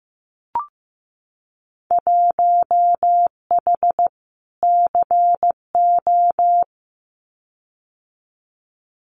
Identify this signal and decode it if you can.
Morse code, heard as a keyed tone, about 15 wpm, 700 Hz. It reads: 1H CO
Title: Winter Field Day Exchanges 15wpm